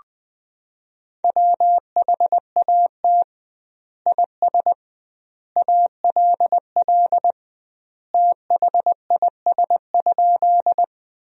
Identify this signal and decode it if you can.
Morse code, heard as a keyed tone, about 20 wpm, 700 Hz. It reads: WHAT IS ALL THIS?
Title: 2x Word-Spacing - Sentences from Top 100 Words - 20wpm